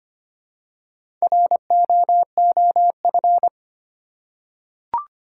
Morse code, heard as a keyed tone, about 25 wpm, 700 Hz. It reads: ROOF
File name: practice / 25wpm / Easy Words with Hints - 25wpm